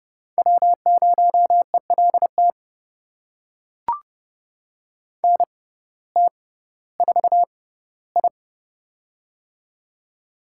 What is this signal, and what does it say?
Morse code, heard as a keyed tone, about 30 wpm, spaced as for 17 wpm, 700 Hz. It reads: W0ELT NT4I
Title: USA Call Signs - 6x Character Spacing - 30wpm